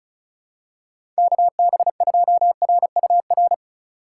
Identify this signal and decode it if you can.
Morse code, heard as a keyed tone, about 36 wpm, 700 Hz. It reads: KB2RUR
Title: USA Call Signs - 4x Character Spacing - 35wpm